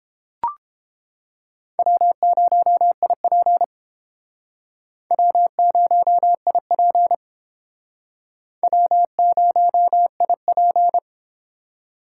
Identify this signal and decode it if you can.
Morse code, heard as a keyed tone, about 31 wpm, 700 Hz. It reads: W0IP W0IP W0IP